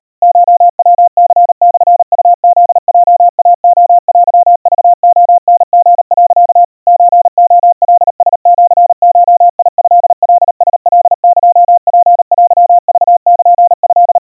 Morse code (Japanese Wabun code), heard as a keyed tone, about 38 wpm, 700 Hz. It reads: コヤニモウフヲイレテクレタリ、 ソレカラシゴトカラカエツテクルト
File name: kawa / Morse sample / kamyu 140 cpm 700 Hz Crispy